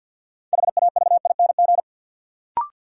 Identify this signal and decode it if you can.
Morse code, heard as a keyed tone, about 51 wpm, 700 Hz. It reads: HAVING
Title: Sets of 5 Words Encore - New Word Drill - 50wpm